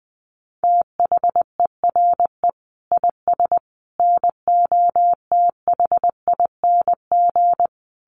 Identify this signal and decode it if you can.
Morse code, heard as a keyed tone, about 20 wpm, 700 Hz. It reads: THERE IS NOTHING
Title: Sentences from Top 500 Words 20wpm